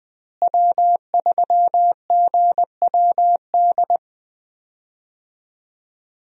Morse code, heard as a keyed tone, about 20 wpm, 700 Hz. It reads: W3GWD
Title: USA Call Signs - 5 Characters - Repeated 2x - 20wpm